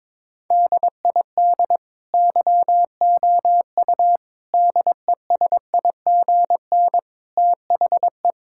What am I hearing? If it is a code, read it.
Morse code, heard as a keyed tone, about 22 wpm, 700 Hz. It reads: DID YOU DESIGN THE
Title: Sentences from Top 1000 Words - Review Part 5 - 22wpm